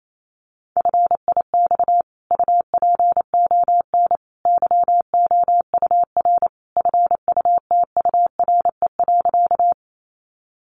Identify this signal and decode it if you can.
Morse code, heard as a keyed tone, about 28 wpm, 700 Hz. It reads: FIX UPON YOUR FUTURE.
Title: Sentences from Top 1400 Words 28wpm